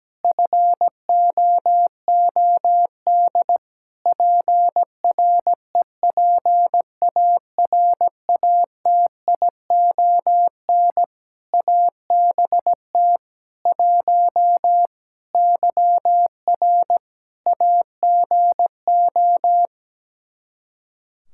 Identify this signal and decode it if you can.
Morse code, heard as a keyed tone, about 17 wpm, 700 Hz. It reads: FOOD PREPARATION ABT 1 YR AGO
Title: Practice QSO - Exchange 3 - More Info - 17wpm